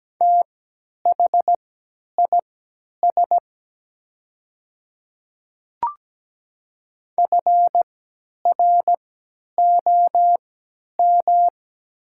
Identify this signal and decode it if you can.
Morse code, heard as a keyed tone, about 17 wpm, spaced as for 10 wpm, 700 Hz. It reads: THIS FROM
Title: 3x Character Spacing - Top 100 Words - 17wpm